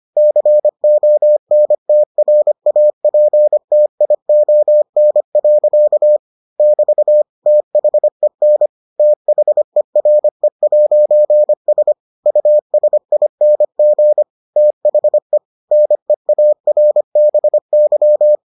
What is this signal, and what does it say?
Morse code, heard as a keyed tone, about 25 wpm, 600 Hz. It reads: CONTRAPTION. <BT> THEN THERE'S USING THE NEARBY